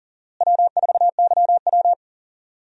Morse code, heard as a keyed tone, about 40 wpm, 700 Hz. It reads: W4YW
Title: USA Call Signs - 4 Characters - Repeated 2x - 40wpm